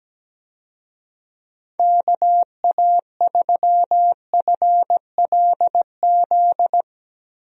Morse code, heard as a keyed tone, about 17 wpm, 700 Hz. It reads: KA3FLZ
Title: USA Call Signs - 6 Characters - 17wpm